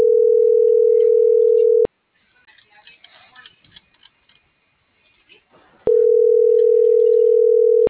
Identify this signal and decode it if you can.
Ringback tone (North America), 440 + 480 Hz.